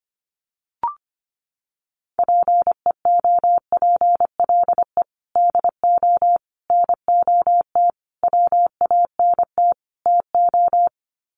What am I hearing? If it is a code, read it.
Morse code, heard as a keyed tone, about 25 wpm, 700 Hz. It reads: PEOPLE DO NOT WANT TO
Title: Sets of 5 Words Encore2 25wpm